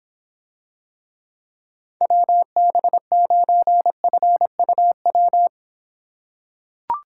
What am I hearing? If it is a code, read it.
Morse code, heard as a keyed tone, about 26 wpm, 700 Hz. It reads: WB9FUW